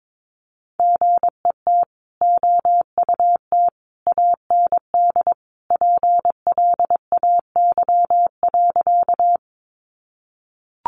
Morse code, heard as a keyed tone, about 22 wpm, 700 Hz. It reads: GET OUT AND PLAY.
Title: Sentences from Top 200 Words 22wpm